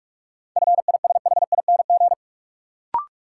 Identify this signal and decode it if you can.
Morse code, heard as a keyed tone, about 43 wpm, 700 Hz. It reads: FISHING
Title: Sentences from Top 1600 Words - New Word Drill Alt - 45wpm